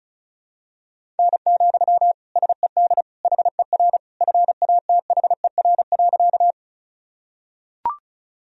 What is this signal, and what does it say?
Morse code, heard as a keyed tone, about 34 wpm, 700 Hz. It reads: N, SED HER FATHER.
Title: Sentences from Top 300 Words and Abbreviations 35wpm